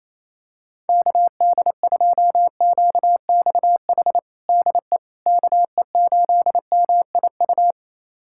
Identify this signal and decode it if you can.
Morse code, heard as a keyed tone, about 28 wpm, 700 Hz. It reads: KD2QXH DE KE8MIU
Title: USA Callsign DE Callsign - 6 Characters - 28wpm